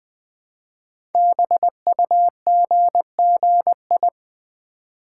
Morse code, heard as a keyed tone, about 20 wpm, 700 Hz. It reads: BUGGI
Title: Code Groups - 5 Letter:Numbers - 20wpm